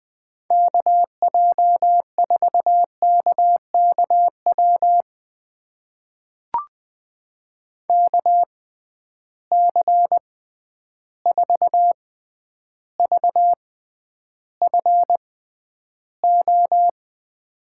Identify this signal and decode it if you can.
Morse code, heard as a keyed tone, about 20 wpm, spaced as for 10 wpm, 700 Hz. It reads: KJ4KKW KC4VFO
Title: USA Call Signs - 6 Characters - 6x Character Spacing - 20wpm